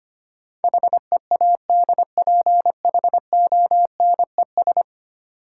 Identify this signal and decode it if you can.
Morse code, heard as a keyed tone, about 25 wpm, 700 Hz. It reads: HEADPHONES
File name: practice / 25wpm / Easy Words with Hints - 25wpm